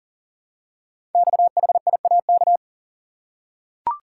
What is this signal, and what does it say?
Morse code, heard as a keyed tone, about 42 wpm, 700 Hz. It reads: XHIAK
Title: Code Groups - 5 Letter:Numbers - 40wpm